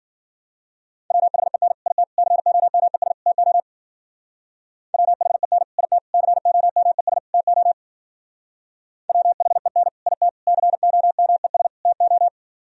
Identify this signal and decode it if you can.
Morse code, heard as a keyed tone, about 50 wpm, 700 Hz. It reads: WHEN IT COMES TO WHEN IT COMES TO WHEN IT COMES TO
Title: Speed-Racing - Sets of 4 Words 40wpm